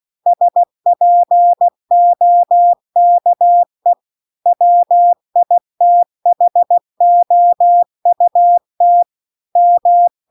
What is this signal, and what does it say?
Morse code, heard as a keyed tone, about 16 wpm, 700 Hz. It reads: SPOKE WITHOUT M